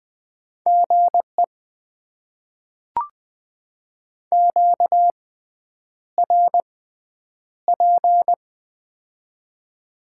Morse code, heard as a keyed tone, about 20 wpm, spaced as for 9 wpm, 700 Hz. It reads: GE QRP
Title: Top 90 QSO Elements - 6x Character Spacing - 20wpm